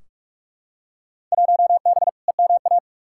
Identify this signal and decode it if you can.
Morse code, heard as a keyed tone, about 42 wpm, 700 Hz. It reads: 1B EMA
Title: Field Day Exchanges - 45wpm